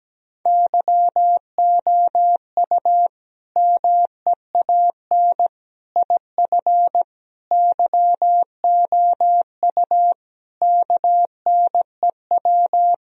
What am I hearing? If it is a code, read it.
Morse code, heard as a keyed tone, about 17 wpm, 700 Hz. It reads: YOU MEAN IF YOU KNEW
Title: Sentences from Top 1000 Words - Review Part 4 - 17wpm